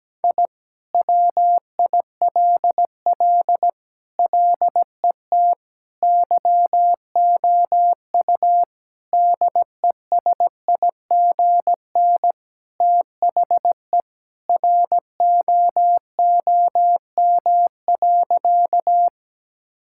Morse code, heard as a keyed tone, about 17 wpm, 700 Hz. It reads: I WILL LET YOU DESIGN THE ROOM.
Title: Sentences from Top 1000 Words - Review Part 5 - 17wpm